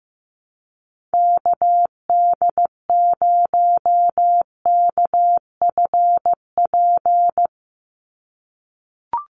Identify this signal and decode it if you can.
Morse code, heard as a keyed tone, about 15 wpm, 700 Hz. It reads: KD0KFP
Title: USA Call Signs 15wpm